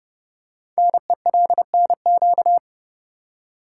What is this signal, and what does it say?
Morse code, heard as a keyed tone, about 30 wpm, 700 Hz. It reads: NELNQ
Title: Code Groups - 5 Letters - 30wpm